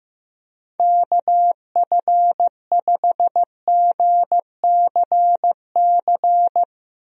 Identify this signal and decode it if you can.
Morse code, heard as a keyed tone, about 15 wpm, 700 Hz. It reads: KF5GCC
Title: USA Call Signs - 6 Characters - Repeated 2x - 15wpm